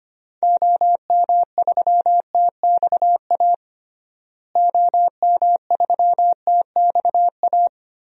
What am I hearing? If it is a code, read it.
Morse code, heard as a keyed tone, about 25 wpm, 700 Hz. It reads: OM3TXA OM3TXA
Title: DXCC-100 Call Signs - 6 Characters - Repeated 2x - 25wpm